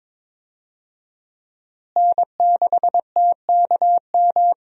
Morse code, heard as a keyed tone, about 22 wpm, 700 Hz. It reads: N6TKM
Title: USA Call Signs - 5 Characters - Repeated 2x - 22wpm